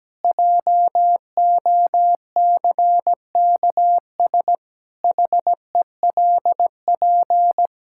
Morse code, heard as a keyed tone, about 17 wpm, 700 Hz. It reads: JOCKS HELP
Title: Pangram Challenge v2 - 17wpm